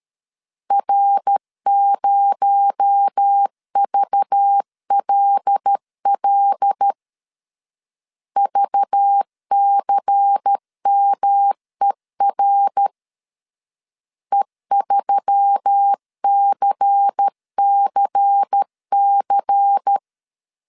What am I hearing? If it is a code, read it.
Morse code, heard as a keyed tone, about 13 wpm, 800 Hz. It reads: R0VLL VCMER E3CCC
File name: CW / DET / 1 6 1500C S B 12wpm 3431